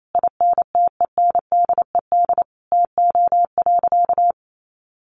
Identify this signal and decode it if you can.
Morse code, heard as a keyed tone, about 28 wpm, 700 Hz. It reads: INTENDED TO.
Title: A Sound Of Thunder 28wpm